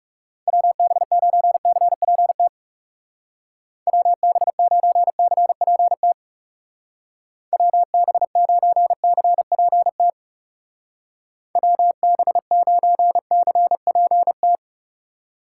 Morse code, heard as a keyed tone, about 36 wpm, 700 Hz. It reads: WB9CPT WB9CPT WB9CPT WB9CPT
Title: Speed Racing - USA 2x3 Call Signs 30wpm